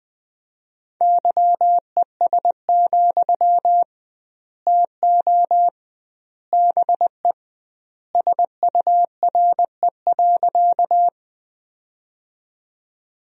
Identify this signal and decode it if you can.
Morse code, heard as a keyed tone, about 20 wpm, 700 Hz. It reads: YES, TO BE SURE.
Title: 2x Word-Spacing - Sentences from Top 500 Words - 20wpm